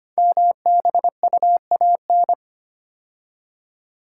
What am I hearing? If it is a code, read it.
Morse code, heard as a keyed tone, about 25 wpm, 700 Hz. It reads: MBUAN
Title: Code Groups - 5 Letters - 25wpm